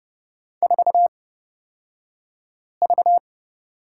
Morse code, heard as a keyed tone, about 30 wpm, spaced as for 5 wpm, 700 Hz. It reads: 4V
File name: Lesson 38d - Letter q - Two Characters 30wpm